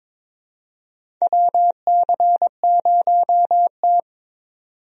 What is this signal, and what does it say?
Morse code, heard as a keyed tone, about 22 wpm, 700 Hz. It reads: WC0T
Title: USA Call Signs - 4 Characters - 22wpm